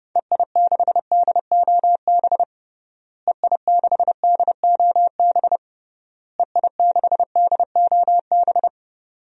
Morse code, heard as a keyed tone, about 30 wpm, 700 Hz. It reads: EI6DOB EI6DOB EI6DOB
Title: DXCC-100 Call Signs - 6 Characters - Repeated 3x - 30wpm